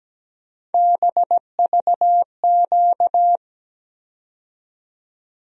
Morse code, heard as a keyed tone, about 17 wpm, 700 Hz. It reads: BVQ
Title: USA Call Sign Postfixes - 3 Letters 17wpm